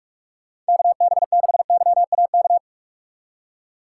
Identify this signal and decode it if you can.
Morse code, heard as a keyed tone, about 60 wpm, spaced as for 52 wpm, 700 Hz. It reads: KB6YAK
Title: Speed Racing - USA Call Signs 45wpm